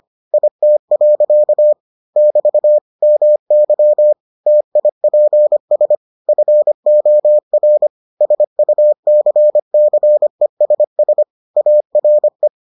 Morse code, heard as a keyed tone, about 25 wpm, 600 Hz. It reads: IT. <BT> MY TIPS FOR SUCCESS ARE